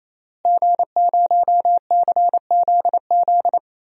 Morse code, heard as a keyed tone, about 28 wpm, 690 Hz. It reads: G0CZZ
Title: DXCC-100 Call Signs - 5 Characters - Repeated 3x - 28wpm